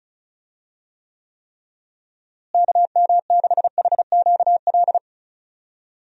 Morse code, heard as a keyed tone, about 36 wpm, 700 Hz. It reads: KM6HQL
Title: USA Call Signs - 6 Characters - Repeated 3x - 35wpm